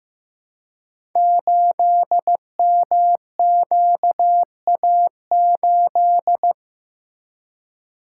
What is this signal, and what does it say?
Morse code, heard as a keyed tone, about 15 wpm, 700 Hz. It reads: 8MQA8
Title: Code Groups - 5 Letter:Numbers - 15wpm